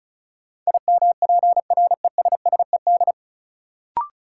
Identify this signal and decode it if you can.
Morse code, heard as a keyed tone, about 36 wpm, 700 Hz. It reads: IMPRESSED
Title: Sentences from Top 1600 Words - New Word Drill Alt - 35wpm